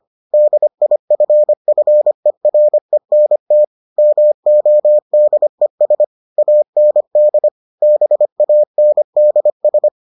Morse code, heard as a keyed tone, about 25 wpm, 600 Hz. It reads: DIFFERENT MODES AND BANDS